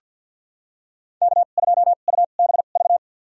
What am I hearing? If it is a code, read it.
Morse code, heard as a keyed tone, about 47 wpm, spaced as for 35 wpm, 700 Hz. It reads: K2UBV
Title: USA Call Signs - 5 Characters - 2x Character Spacing - 50wpm